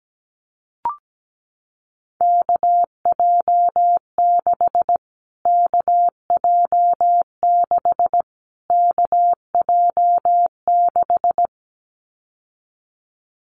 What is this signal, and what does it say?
Morse code, heard as a keyed tone, about 17 wpm, 700 Hz. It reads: KJ6 KJ6 KJ6
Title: Callsign Prefixes for Top 25 DXCC - Repeated 3x - 17wpm